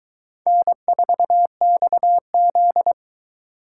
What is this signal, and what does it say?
Morse code, heard as a keyed tone, about 21 wpm, 700 Hz. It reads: N4XZ